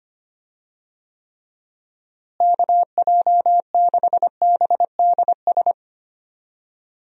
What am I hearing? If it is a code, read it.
Morse code, heard as a keyed tone, about 25 wpm, 700 Hz. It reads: KJ6BDS
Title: USA Call Signs - 6 Characters - Repeated 2x - 25wpm